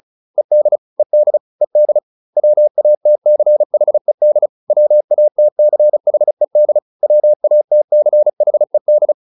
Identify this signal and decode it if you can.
Morse code, heard as a keyed tone, about 34 wpm, 600 Hz. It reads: ED ED ED WATCHED WATCHED WATCHED